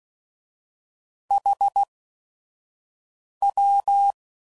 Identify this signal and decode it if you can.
Morse code, heard as a keyed tone, about 16 wpm, spaced as for 5 wpm, 800 Hz. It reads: HW
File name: CW Code Groups @5WPM / QSOs / QSO WA7QOX 1